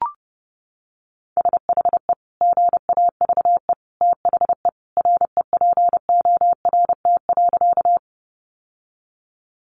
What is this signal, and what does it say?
Morse code, heard as a keyed tone, about 30 wpm, 700 Hz. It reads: SHE GAVE THE REPORT.
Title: Sentences from Top 900 Words 30wpm